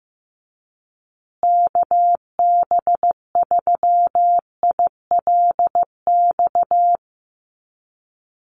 Morse code, heard as a keyed tone, about 15 wpm, 700 Hz. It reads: KB3ILX
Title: USA Call Signs Repeated 3x 15wpm